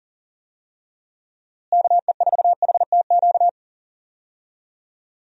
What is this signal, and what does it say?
Morse code, heard as a keyed tone, about 40 wpm, 700 Hz. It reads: KE4HTQ
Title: USA Call Signs - 6 Characters - Repeated 2x - 40wpm